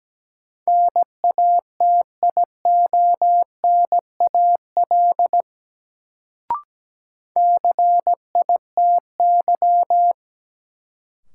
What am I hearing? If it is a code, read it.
Morse code, heard as a keyed tone, about 17 wpm, 700 Hz. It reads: NATIONAL CITY
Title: Sets of 2 Words - New Word Drill - 17wpm